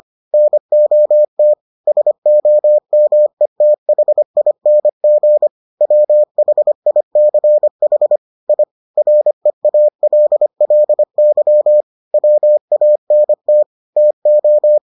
Morse code, heard as a keyed tone, about 25 wpm, 600 Hz. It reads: NOT SOMETHING WHICH I REALLY WANT TO